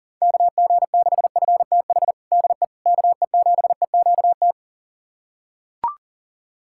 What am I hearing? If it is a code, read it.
Morse code, heard as a keyed tone, about 40 wpm, 690 Hz. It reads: KG6FTH DE KE7EQT
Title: USA Callsign DE Callsign - 6 Characters - 40wpm